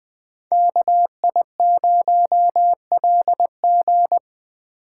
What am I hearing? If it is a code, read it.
Morse code, heard as a keyed tone, about 20 wpm, 700 Hz. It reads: KI0LG